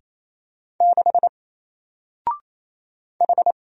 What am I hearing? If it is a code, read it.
Morse code, heard as a keyed tone, about 27 wpm, 700 Hz. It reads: 6 H
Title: Single Letter-Number Alt 28wpm